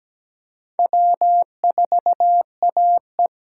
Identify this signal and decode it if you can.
Morse code, heard as a keyed tone, about 17 wpm, 700 Hz. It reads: W4AE
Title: USA Call Signs - 8x Character Spacing - 17wpm